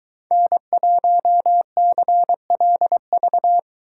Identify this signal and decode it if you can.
Morse code, heard as a keyed tone, about 23 wpm, 700 Hz. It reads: N1CLV